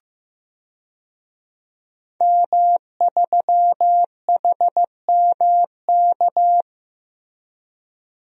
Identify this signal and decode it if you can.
Morse code, heard as a keyed tone, about 15 wpm, 700 Hz. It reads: M3HMK
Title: DXCC-100 Call Signs - 5 Characters - Repeated 2x - 15wpm